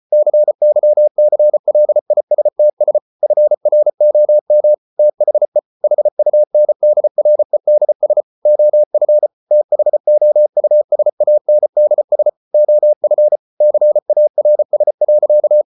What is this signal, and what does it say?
Morse code, heard as a keyed tone, about 34 wpm, 600 Hz. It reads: CYCLISTS FROM THE HUNDREDS OF THOUSANDS OF CARS.